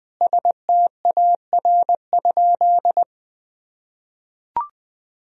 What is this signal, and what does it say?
Morse code, heard as a keyed tone, about 20 wpm, 700 Hz. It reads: STAR?
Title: Sentences from Top 1000 Words - Review Part 2 - 20wpm